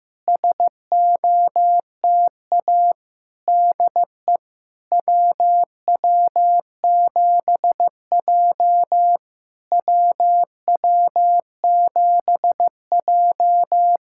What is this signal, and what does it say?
Morse code, heard as a keyed tone, about 15 wpm, 700 Hz. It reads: SOTA DE WW7J WW7J
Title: SOTA QSO Practice 15wpm